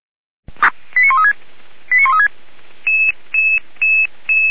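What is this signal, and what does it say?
Five-tone selective call (ZVEI), heard as quick runs of five tones, tones of about 70 ms. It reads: 89136 89136
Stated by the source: ZVEI five-tone call received by SDR, Feuerwehr Orlamünde.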